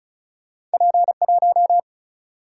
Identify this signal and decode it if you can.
Morse code, heard as a keyed tone, about 36 wpm, 700 Hz. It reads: P 1
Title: Single-Letter Single-Number 35wpm